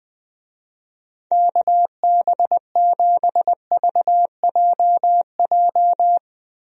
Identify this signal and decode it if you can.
Morse code, heard as a keyed tone, about 20 wpm, 700 Hz. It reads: KB7VJJ
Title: USA Call Signs - 6 Characters - Repeated 3x - 20wpm